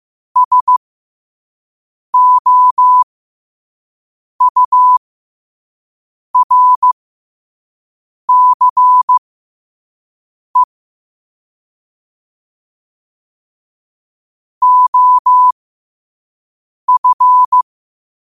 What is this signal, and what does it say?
Morse code, heard as a keyed tone, about 15 wpm, spaced as for 5 wpm, 1000 Hz. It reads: SOURCE OF
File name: 5wpm